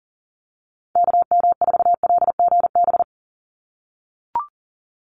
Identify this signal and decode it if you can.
Morse code, heard as a keyed tone, about 40 wpm, 700 Hz. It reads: KM4LGB
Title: USA Call Signs 40wpm